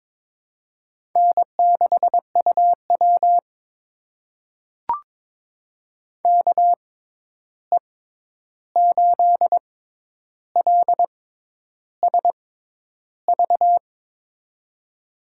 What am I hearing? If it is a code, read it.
Morse code, heard as a keyed tone, about 21 wpm, spaced as for 10 wpm, 700 Hz. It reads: N6UW KE8LSV
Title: USA Call Signs - 6x Character Spacing - 22wpm